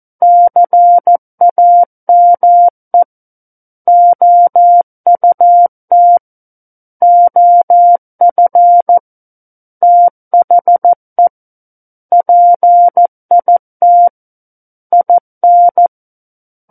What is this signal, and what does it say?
Morse code, heard as a keyed tone, about 14 wpm, 700 Hz. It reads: CAME OUT OF THE PIT IN
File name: Book One / 05-Chapter 5 - 14 WPM